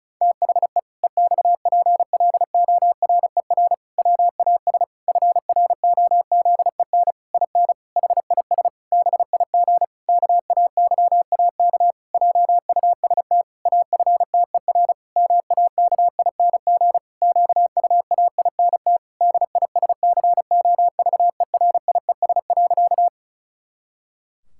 Morse code, heard as a keyed tone, about 34 wpm, 700 Hz. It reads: THE EXPLORER WAS FROZEN IN HIS BIG KAYAK JUST AFTER MAKING QUAINT DISCOVERIES.
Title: Pangram Challenge v2 - 35wpm